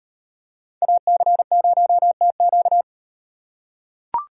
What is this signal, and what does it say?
Morse code, heard as a keyed tone, about 38 wpm, 700 Hz. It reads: AC0TQ